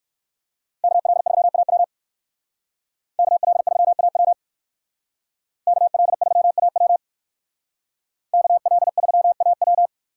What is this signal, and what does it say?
Morse code, heard as a keyed tone, about 61 wpm, 700 Hz. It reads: KL3AW KL3AW KL3AW KL3AW